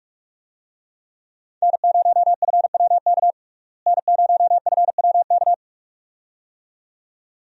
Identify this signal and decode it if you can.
Morse code, heard as a keyed tone, about 47 wpm, 700 Hz. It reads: N0FWK N0FWK
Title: USA Call Signs - 5 Characters - Repeated 2x - 45wpm